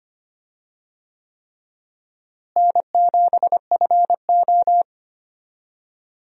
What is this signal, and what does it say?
Morse code, heard as a keyed tone, about 25 wpm, 700 Hz. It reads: N7FO
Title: USA Call Signs - 4 Characters - Repeated 3x - 25wpm